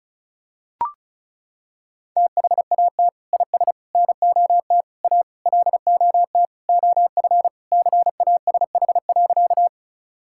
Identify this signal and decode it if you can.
Morse code, heard as a keyed tone, about 36 wpm, 700 Hz. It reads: THAT IS NOT A LOT OF CASH.
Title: Sentences from Top 1000 Words - Review Part 5 - 35wpm